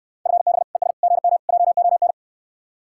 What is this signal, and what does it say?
Morse code, heard as a keyed tone, about 69 wpm, 700 Hz. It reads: FB ES GM JON